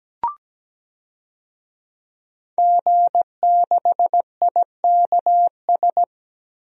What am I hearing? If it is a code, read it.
Morse code, heard as a keyed tone, about 17 wpm, 700 Hz. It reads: G6IKS
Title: DXCC-100 Call Signs - 5 Characters - Repeated 2x - 17wpm